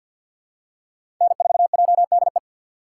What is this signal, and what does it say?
Morse code, heard as a keyed tone, about 51 wpm, 700 Hz. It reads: N4JDE
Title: USA Call Signs - 5 Characters - 3x Character Spacing - 50wpm